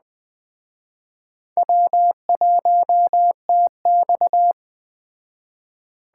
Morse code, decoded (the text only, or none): W1TX